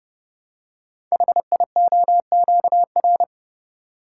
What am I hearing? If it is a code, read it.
Morse code, heard as a keyed tone, about 30 wpm, 700 Hz. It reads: HIOQR
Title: Code Groups - 5 Letter:Numbers - 30wpm